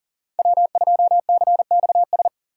Morse code, heard as a keyed tone, about 42 wpm, 700 Hz. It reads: W2CXS
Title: Code Groups - 5 Letter:Numbers - 40wpm